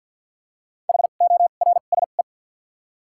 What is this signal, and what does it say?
Morse code, heard as a keyed tone, about 53 wpm, spaced as for 37 wpm, 700 Hz. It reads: HORSE
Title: 2x Character Spacing - Top 300 Words - 50wpm